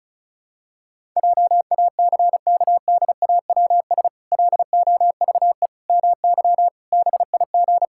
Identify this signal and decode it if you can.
Morse code, heard as a keyed tone, about 34 wpm, 700 Hz. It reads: JACKDAWS LOVE MY BIG
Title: Pangram Challenge v2 - 35wpm